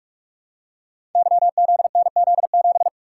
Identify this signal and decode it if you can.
Morse code, heard as a keyed tone, about 44 wpm, 690 Hz. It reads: YZNZ7
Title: Code Groups - 5 Letter:Numbers - 45wpm